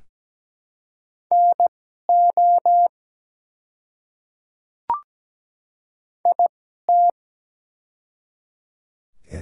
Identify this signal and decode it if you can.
Morse code, heard as a keyed tone, about 17 wpm, spaced as for 12 wpm, 700 Hz. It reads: NO IT
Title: 2x Character Spacing - Top 200 Words - 17wpm